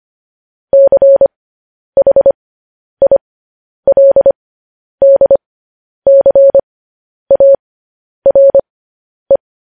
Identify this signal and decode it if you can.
Morse code, heard as a keyed tone, about 25 wpm, spaced as for 10 wpm, 550 Hz. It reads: CHILDCARE
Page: News Headlines in Morse Code at 10 WPM